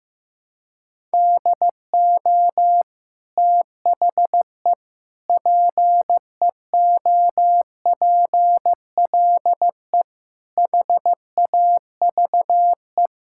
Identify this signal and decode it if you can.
Morse code, heard as a keyed tone, about 15 wpm, 700 Hz. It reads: DO THE PEOPLE HAVE
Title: Sentences from Top 1000 Words - Review Part 5 - 15wpm